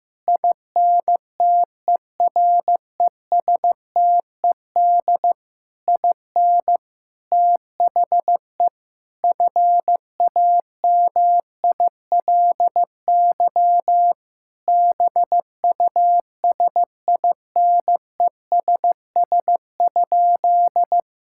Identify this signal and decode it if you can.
Morse code, heard as a keyed tone, about 15 wpm, 700 Hz. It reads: INTERESTED IN THE FAMILY BUSINESS?
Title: Sentences from Top 1000 Words - Review Part 6 - 15wpm